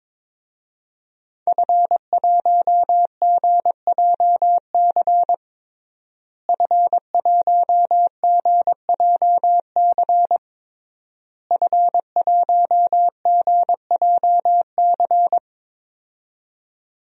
Morse code, heard as a keyed tone, about 22 wpm, 700 Hz. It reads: F1GJC F1GJC F1GJC
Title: DXCC-100 Call Signs - 5 Characters - Repeated 3x - 22wpm